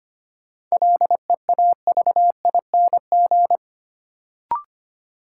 Morse code, heard as a keyed tone, about 25 wpm, 700 Hz. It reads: LEAVING
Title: Sentences from Top 1000 Words - Review Part 4 - New Word Drill - 25wpm